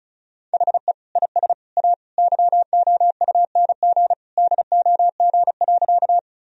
Morse code, heard as a keyed tone, about 35 wpm, 700 Hz. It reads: HE IS A YOUNG DOG.
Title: Sentences from Top 1000 Words - Review Part 2 - 35wpm